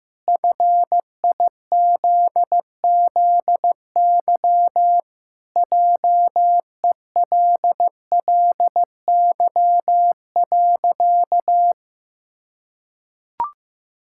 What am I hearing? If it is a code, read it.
Morse code, heard as a keyed tone, about 15 wpm, 700 Hz. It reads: FIZZY JELLY.
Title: Pangram Challenge v2 - 15wpm